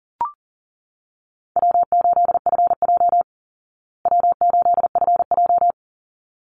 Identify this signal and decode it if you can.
Morse code, heard as a keyed tone, about 40 wpm, 700 Hz. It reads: W8FJ W8FJ
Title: Easy Call Signs Repeated 2x - 40wpm